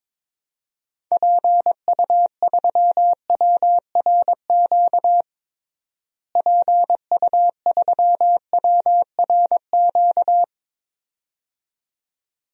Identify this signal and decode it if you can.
Morse code, heard as a keyed tone, about 21 wpm, 700 Hz. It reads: PU3WRQ PU3WRQ
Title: DXCC-100 Call Signs - 6 Characters - Repeated 2x - 22wpm